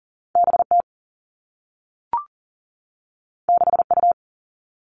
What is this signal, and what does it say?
Morse code, heard as a keyed tone, about 42 wpm, 700 Hz. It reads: B T 6 U
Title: Two Challenging Letter-Number Combinations 40wpm